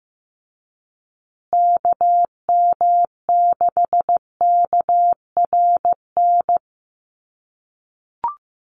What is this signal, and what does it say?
Morse code, heard as a keyed tone, about 15 wpm, 700 Hz. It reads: KM6KRN
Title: USA Call Signs Repeated 2x 15wpm